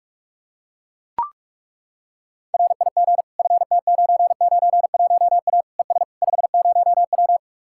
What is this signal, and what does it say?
Morse code, heard as a keyed tone, about 40 wpm, 700 Hz. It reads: RIG FT991A ES 50W